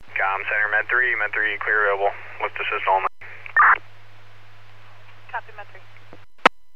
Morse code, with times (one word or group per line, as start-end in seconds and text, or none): none